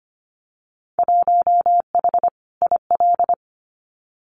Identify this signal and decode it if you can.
Morse code, heard as a keyed tone, about 25 wpm, 700 Hz. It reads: 1H IL
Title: Winter Field Day Exchanges 25wpm